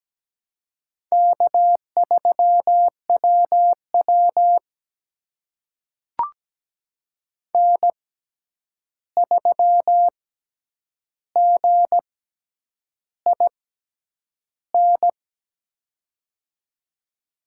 Morse code, heard as a keyed tone, about 17 wpm, spaced as for 8 wpm, 700 Hz. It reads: K3WW N3GIN
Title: USA Call Signs - 6x Character Spacing - 17wpm